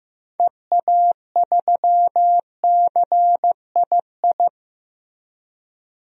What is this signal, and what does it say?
Morse code, heard as a keyed tone, about 15 wpm, 700 Hz. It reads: EA3CII
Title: DXCC-100 Call Signs - 6 Characters - Repeated 2x - 15wpm